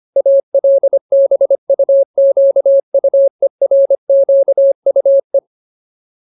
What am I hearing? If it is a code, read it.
Morse code, heard as a keyed tone, about 25 wpm, 550 Hz. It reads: ALBUQUERQUE
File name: Ultimate Word List / 25WPM Ultimate